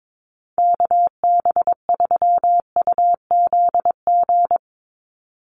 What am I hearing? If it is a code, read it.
Morse code, heard as a keyed tone, about 22 wpm, 700 Hz. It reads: KB3UZG
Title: USA Call Signs 22wpm